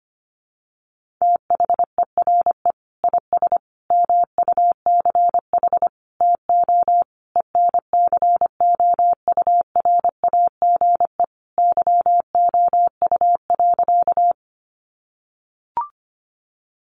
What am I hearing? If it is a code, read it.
Morse code, heard as a keyed tone, about 25 wpm, 700 Hz. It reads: THERE IS MUCH TO ENCOURAGE YOU.
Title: Sentences from Top 1400 Words 25wpm